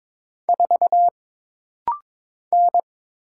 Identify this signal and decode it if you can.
Morse code, heard as a keyed tone, about 22 wpm, 700 Hz. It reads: 4 N
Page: This rapid-fire timing is geared to push your ICR (Instant Character Recognition) skills.